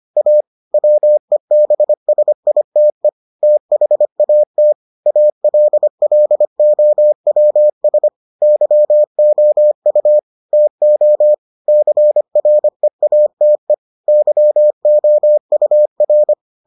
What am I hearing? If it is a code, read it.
Morse code, heard as a keyed tone, about 25 wpm, 600 Hz. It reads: A WEBSITE THAT ALLOWS YOU TO CREATE YOUR